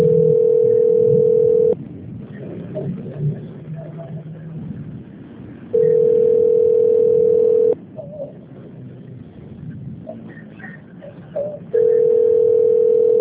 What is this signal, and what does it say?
Ringback tone (North America), 440 + 480 Hz.
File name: rg-1600-unknown- - - .12956